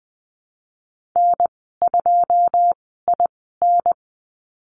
Morse code, heard as a keyed tone, about 20 wpm, spaced as for 15 wpm, 700 Hz. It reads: N2IN
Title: 173 Lesson 28g - Number 9 - Call Signs 20wpm